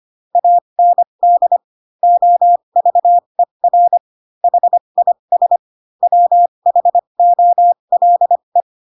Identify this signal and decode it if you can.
Morse code, heard as a keyed tone, about 25 wpm, spaced as for 22 wpm, 700 Hz. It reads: AND OVER HIS WHOLE
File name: Pinocchio / Pinocchio ch6 18 25 ik1hge